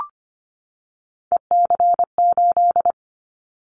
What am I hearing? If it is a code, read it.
Morse code, heard as a keyed tone, about 25 wpm, 700 Hz. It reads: EC8
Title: Callsign Prefixes for Top 100 DXCC - 25wpm